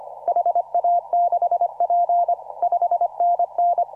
Morse code, heard as a keyed tone, about 24 wpm, 690 Hz. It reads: HA6P 5NN
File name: CW / RHOB CW 2017 HA6P 0001